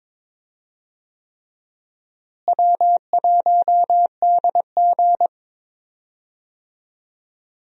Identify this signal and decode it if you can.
Morse code, heard as a keyed tone, about 21 wpm, 700 Hz. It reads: W1DG
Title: USA Call Signs - 4 Characters - Repeated 3x - 22wpm